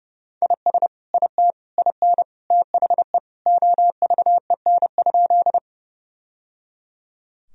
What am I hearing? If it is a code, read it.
Morse code, heard as a keyed tone, about 30 wpm, 700 Hz. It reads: IS IT IN THE OVEN?
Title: Sentences from Top 1000 Words - Review Part 4 - 30wpm